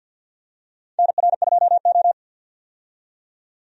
Morse code, heard as a keyed tone, about 49 wpm, 700 Hz. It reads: NR2O
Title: USA Call Signs - 4 Characters - 50wpm